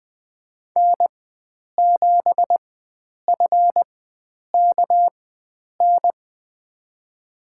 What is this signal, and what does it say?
Morse code, heard as a keyed tone, about 20 wpm, spaced as for 9 wpm, 700 Hz. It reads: N7FKN
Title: USA Call Signs - 5 Characters - 4x Character Spacing - 20wpm